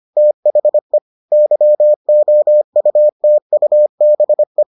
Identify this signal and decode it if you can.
Morse code, heard as a keyed tone, about 25 wpm, 600 Hz. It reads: THE YOUTUBE